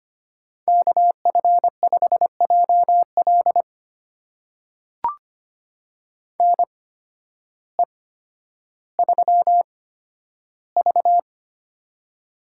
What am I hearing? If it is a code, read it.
Morse code, heard as a keyed tone, about 25 wpm, spaced as for 12 wpm, 700 Hz. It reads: KF5JL NE3V
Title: USA Call Signs - 8x Character Spacing - 25wpm